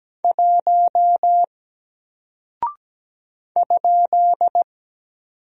Morse code, heard as a keyed tone, about 17 wpm, 700 Hz. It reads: 1 ?
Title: Single Character (Letter:Number:Punctuation:Prosign) 17wpm